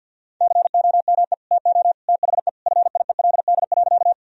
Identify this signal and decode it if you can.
Morse code, heard as a keyed tone, about 50 wpm, 700 Hz. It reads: COME TO THE FIELD.
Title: Sentences from Top 1000 Words - Review Part 2 - 50wpm